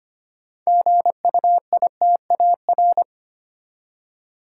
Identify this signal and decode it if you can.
Morse code, heard as a keyed tone, about 25 wpm, 690 Hz. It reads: GUITAR
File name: practice / 25wpm / Easy Words with Hints - 25wpm